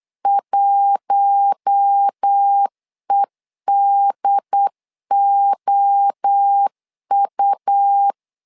Morse code, heard as a keyed tone, about 8 wpm, 800 Hz. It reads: 1EDOU
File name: LEZIONI / 4 Q1 8wpm 3035